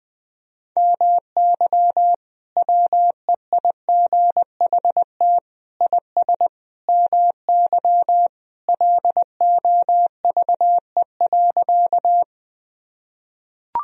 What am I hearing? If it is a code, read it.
Morse code, heard as a keyed tone, about 19 wpm, 700 Hz. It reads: MY WEIGHT IS MY LOVE.
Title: Sentences from Top 1000 Words - Review Part 3 - 20wpm